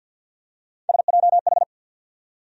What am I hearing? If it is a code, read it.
Morse code, heard as a keyed tone, about 52 wpm, 700 Hz. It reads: SJH
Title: USA Call Sign Postfixes - 3 Letters 50wpm